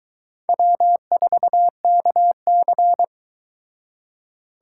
Morse code, heard as a keyed tone, about 23 wpm, 700 Hz. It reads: W4KC